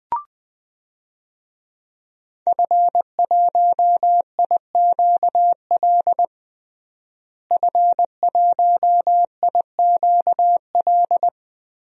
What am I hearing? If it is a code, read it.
Morse code, heard as a keyed tone, about 21 wpm, 700 Hz. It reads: F1IQL F1IQL
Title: DXCC-100 Call Signs - 5 Characters - Repeated 2x - 20wpm